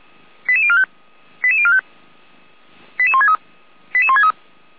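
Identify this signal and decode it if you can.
Five-tone selective call (ZVEI), heard as quick runs of five tones, tones of about 70 ms. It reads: 88045 88045 88163 88163
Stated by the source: ZVEI five-tone call received by SDR, from the feuerwehr-orlamuende.org fire brigade audio archive.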